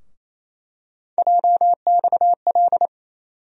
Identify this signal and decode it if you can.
Morse code, heard as a keyed tone, about 28 wpm, 700 Hz. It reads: JXL
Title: USA Call Sign Postfixes - 3 Letters 28wpm